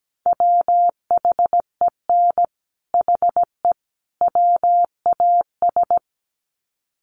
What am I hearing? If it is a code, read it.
Morse code, heard as a keyed tone, about 17 wpm, 700 Hz. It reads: WHEN HE WAS
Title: Sets of 3 Words 17wpm